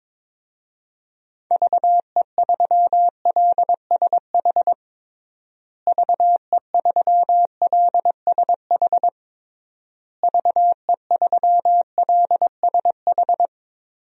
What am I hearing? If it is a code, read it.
Morse code, heard as a keyed tone, about 22 wpm, 700 Hz. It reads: VE3LSH VE3LSH VE3LSH
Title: DXCC-100 Call Signs - 6 Characters - Repeated 3x - 22wpm